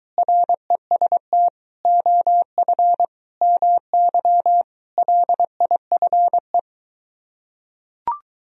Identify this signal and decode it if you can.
Morse code, heard as a keyed tone, about 23 wpm, 700 Hz. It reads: REST OF MY LIFE